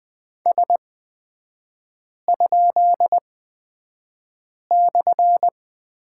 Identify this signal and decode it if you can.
Morse code, heard as a keyed tone, about 20 wpm, spaced as for 5 wpm, 700 Hz. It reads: S?/
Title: 224 Lesson 36e - BT - Three Characters 20wpm